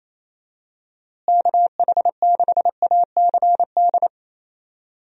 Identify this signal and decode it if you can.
Morse code, heard as a keyed tone, about 28 wpm, 700 Hz. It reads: KH6ACD
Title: USA Call Signs - 6 Characters - Repeated 3x - 28wpm